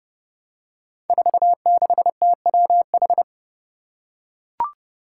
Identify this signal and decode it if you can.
Morse code, heard as a keyed tone, about 30 wpm, 700 Hz. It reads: 46TWH
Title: Five Challenging Letter-Number Combinations 30wpm